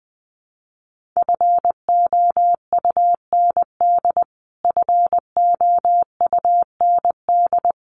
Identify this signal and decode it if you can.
Morse code, heard as a keyed tone, about 20 wpm, 700 Hz. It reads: FOUND FOUND
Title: Top 500 Words Repeated 20wpm